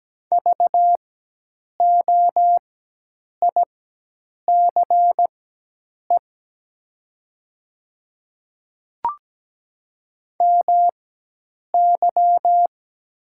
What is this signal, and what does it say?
Morse code, heard as a keyed tone, about 17 wpm, spaced as for 8 wpm, 700 Hz. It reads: VOICE MY
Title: 4x Character Spacing - Top 500 Words - 17wpm